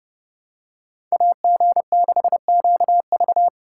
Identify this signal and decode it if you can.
Morse code, heard as a keyed tone, about 28 wpm, 700 Hz. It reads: AG6QV